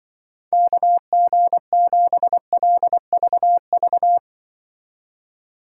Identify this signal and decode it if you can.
Morse code, heard as a keyed tone, about 24 wpm, 700 Hz. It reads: KG7LVV
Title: Speed Racing - USA 2x3 Call Signs 20wpm